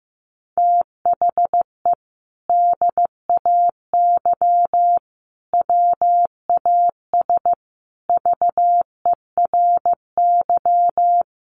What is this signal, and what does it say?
Morse code, heard as a keyed tone, about 15 wpm, 700 Hz. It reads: THE DAY WAS VERY
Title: Sentences from Top 1500 Words 15wpm